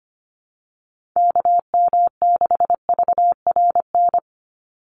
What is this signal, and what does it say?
Morse code, heard as a keyed tone, about 25 wpm, 700 Hz. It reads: KM6VRN
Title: USA Call Signs 25wpm